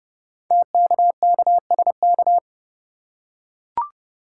Morse code, heard as a keyed tone, about 29 wpm, 700 Hz. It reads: TKKSK
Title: Code Groups - 5 Letters - 30wpm